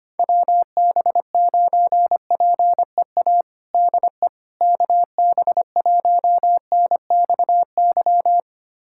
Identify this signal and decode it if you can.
Morse code, heard as a keyed tone, about 25 wpm, 700 Hz. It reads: WB9PEA DE KB1NXY
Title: USA Callsign DE Callsign - 6 Characters - 25wpm